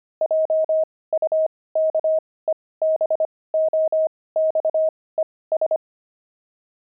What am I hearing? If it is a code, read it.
Morse code, heard as a keyed tone, about 25 wpm, spaced as for 18 wpm, 630 Hz. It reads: JUKEBOXES
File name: pangram-fs-18-cs-25